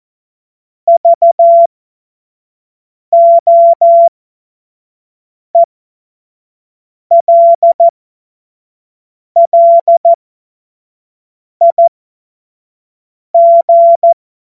Morse code, heard as a keyed tone, about 14 wpm, spaced as for 5 wpm, 680 Hz. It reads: VOELLIG
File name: Witz-23-September-2025-25-F70